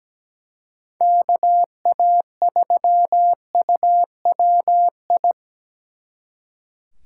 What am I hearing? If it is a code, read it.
Morse code, heard as a keyed tone, about 17 wpm, 700 Hz. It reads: KA3UWI
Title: Speed Racing - USA 2x3 Call Signs 17wpm